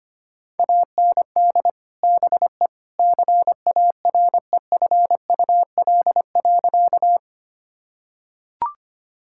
Morse code, heard as a keyed tone, about 25 wpm, 700 Hz. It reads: AND BE CAREFUL.
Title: Sentences from Top 1600 Words 25wpm